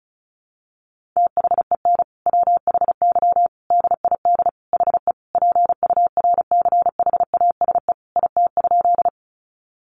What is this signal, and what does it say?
Morse code, heard as a keyed tone, about 34 wpm, 700 Hz. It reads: THEN WHY DID HE PURCHASE IT?
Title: Sentences from Top 800 Words 35wpm